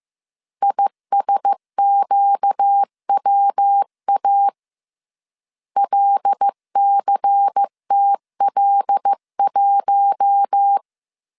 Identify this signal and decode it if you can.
Morse code, heard as a keyed tone, about 15 wpm, 800 Hz. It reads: ISQWA LCTL1
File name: CW / DET / 1 6 1500C S C 14wpm 2951